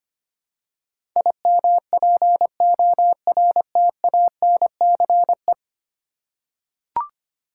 Morse code, heard as a keyed tone, about 25 wpm, 700 Hz. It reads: IMPORTANCE
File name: 10 Letter Words - First 3 Letters Given - 25wpm